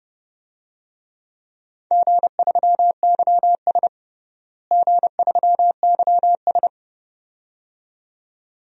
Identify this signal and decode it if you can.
Morse code, heard as a keyed tone, about 32 wpm, 700 Hz. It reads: G3YS G3YS
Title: DXCC-100 Call Signs - 4 Characters - Repeated 2x - 30wpm